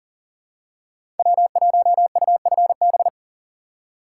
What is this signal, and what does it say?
Morse code, heard as a keyed tone, about 40 wpm, 700 Hz. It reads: W1UFB
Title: USA Call Signs - 5 Characters - 2x Character Spacing - 40wpm